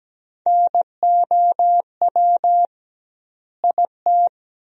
Morse code, heard as a keyed tone, about 17 wpm, 700 Hz. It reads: NOW IT
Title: 2x Word-Spacing - Sentences from Top 200 Words - 17wpm